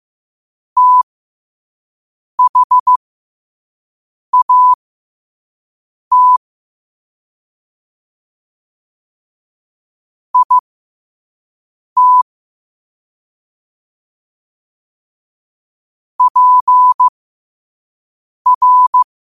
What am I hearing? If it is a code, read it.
Morse code, heard as a keyed tone, about 15 wpm, spaced as for 5 wpm, 1000 Hz. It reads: THAT IT PR
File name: qotdfiles / 5wpm